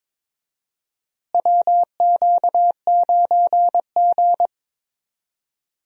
Morse code, heard as a keyed tone, about 22 wpm, 700 Hz. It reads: WQ9G
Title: USA Call Signs - 4 Characters - Repeated 3x - 22wpm